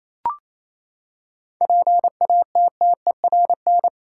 Morse code, heard as a keyed tone, about 27 wpm, 700 Hz. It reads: PATTERN
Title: Seven Letter Words - 28wpm